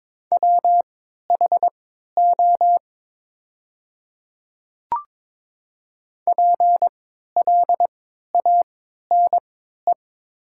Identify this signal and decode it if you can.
Morse code, heard as a keyed tone, about 22 wpm, spaced as for 12 wpm, 700 Hz. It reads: WHO PLANE
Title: 3x Character Spacing - Top 500 Words - 22wpm